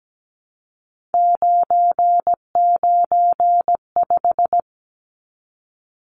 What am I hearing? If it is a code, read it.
Morse code, heard as a keyed tone, about 17 wpm, 700 Hz. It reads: 995
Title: Numbers - 3-Digit 17wpm